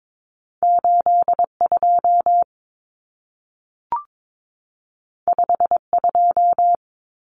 Morse code, heard as a keyed tone, about 22 wpm, 700 Hz. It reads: 82 52
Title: Common Cut Numbers - 2-Digit 22wpm